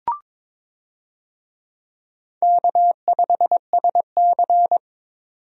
Morse code, heard as a keyed tone, about 21 wpm, 700 Hz. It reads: K5SC
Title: USA Call Signs - 4 Characters - 22wpm